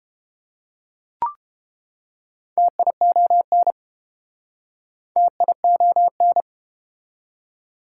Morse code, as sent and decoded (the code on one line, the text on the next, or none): - .. --- -. / - .. --- -.
TION TION